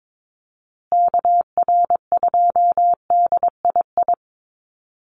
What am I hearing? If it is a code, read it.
Morse code, heard as a keyed tone, about 22 wpm, 700 Hz. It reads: KR2DII
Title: USA Call Signs Repeated 2x 22wpm